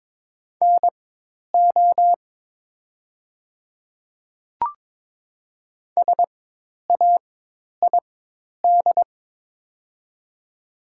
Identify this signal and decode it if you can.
Morse code, heard as a keyed tone, about 22 wpm, spaced as for 10 wpm, 690 Hz. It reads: NO SAID
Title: 4x Character Spacing - Top 500 Words - 22wpm